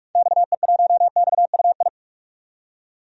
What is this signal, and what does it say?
Morse code, heard as a keyed tone, about 44 wpm, 690 Hz. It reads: XE1XUI
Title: DXCC-100 Call Signs - 6 Characters - 45wpm